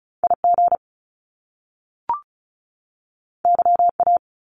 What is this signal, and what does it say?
Morse code, heard as a keyed tone, about 36 wpm, 700 Hz. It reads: IG YA
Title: Top Two Letter Combinations in Words 35wpm